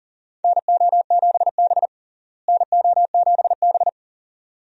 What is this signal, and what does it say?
Morse code, heard as a keyed tone, about 40 wpm, 700 Hz. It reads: NO7B NO7B
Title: USA Call Signs - 4 Characters - Repeated 2x - 40wpm